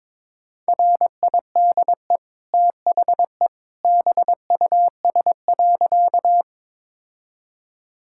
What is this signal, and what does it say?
Morse code, heard as a keyed tone, about 21 wpm, 690 Hz. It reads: RIDE THE BUS.
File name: practice / 22wpm / Easy Sentences with Hints - 22wpm